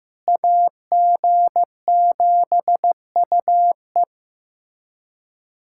Morse code, heard as a keyed tone, about 15 wpm, 700 Hz. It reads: AG7UE
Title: USA Callsign DE Callsign - 5 Characters - 15wpm